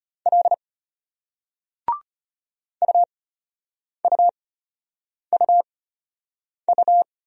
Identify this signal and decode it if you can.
Morse code, heard as a keyed tone, about 34 wpm, spaced as for 8 wpm, 700 Hz. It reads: L UUUU